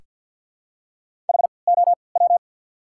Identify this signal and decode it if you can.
Morse code, heard as a keyed tone, about 53 wpm, spaced as for 29 wpm, 700 Hz. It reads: HOW
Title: 3x Character Spacing - Top 200 Words - 50wpm